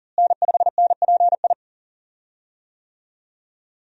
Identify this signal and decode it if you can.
Morse code, heard as a keyed tone, about 40 wpm, 700 Hz. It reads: N5NPI
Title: USA Call Signs - 5 Characters - Repeated 2x - 40wpm